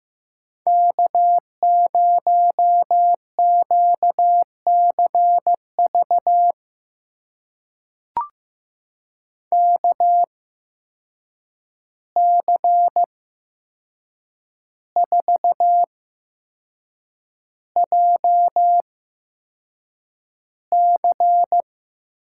Morse code, heard as a keyed tone, about 15 wpm, spaced as for 6 wpm, 700 Hz. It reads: K0QCV KC4JC
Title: USA Call Signs - 5 Characters - 8x Character Spacing - 15wpm